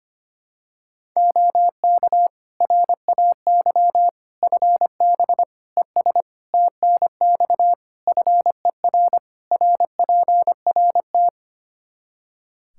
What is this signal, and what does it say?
Morse code, heard as a keyed tone, about 25 wpm, 700 Hz. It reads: OK RAY FB ES TNX FER RPRT
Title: Practice QSO - Exchange 2 - Info - 25wpm